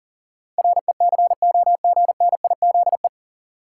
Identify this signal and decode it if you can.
Morse code, heard as a keyed tone, about 40 wpm, 700 Hz. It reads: RECOGNIZE
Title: Sentences from Top 1400 Words - New Word Drill - 40wpm